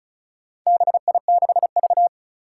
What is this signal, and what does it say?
Morse code, heard as a keyed tone, about 36 wpm, 700 Hz. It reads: BI6V
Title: Four Challenging Letter-Number Combinations 35wpm